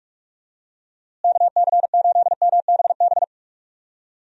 Morse code, heard as a keyed tone, about 42 wpm, 700 Hz. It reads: KC8MBB
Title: USA Call Signs - 6 Characters - Repeated 2x - 45wpm